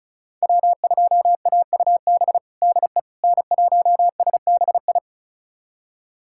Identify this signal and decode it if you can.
Morse code, heard as a keyed tone, about 34 wpm, 700 Hz. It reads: W2AUB DE N1SBI
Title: USA Callsign DE Callsign 35wpm